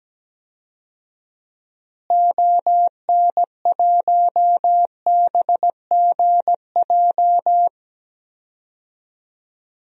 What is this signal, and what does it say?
Morse code, heard as a keyed tone, about 17 wpm, 700 Hz. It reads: ON1BGJ
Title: DXCC-100 Call Signs - 6 Characters - Repeated 2x - 17wpm